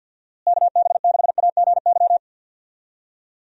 Morse code, heard as a keyed tone, about 50 wpm, 700 Hz. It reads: KB6AGY
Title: USA Call Signs - 6 Characters - Repeated 3x - 50wpm